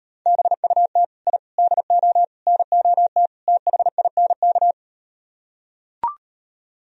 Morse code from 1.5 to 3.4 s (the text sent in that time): DO NOT